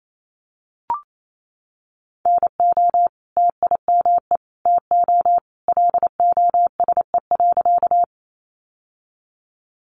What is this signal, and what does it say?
Morse code, heard as a keyed tone, about 28 wpm, 700 Hz. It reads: NO TIME TO LOSE.
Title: Sentences from Top 1100 Words 28wpm